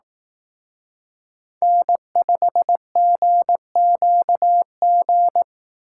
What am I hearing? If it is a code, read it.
Morse code, heard as a keyed tone, about 17 wpm, 700 Hz. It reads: N5GQG
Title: Speed Racing - USA 1x3 Call Signs 15wpm